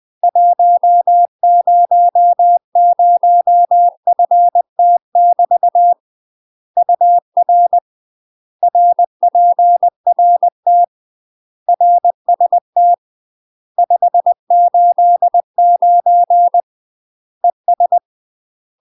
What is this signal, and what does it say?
Morse code, heard as a keyed tone, about 20 wpm, 700 Hz. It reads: 100FT <BT> UR RPRT RST 589 ES
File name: UK MASTER QSOs TRAINING / 20WPM / UK MASTER QSOs 0002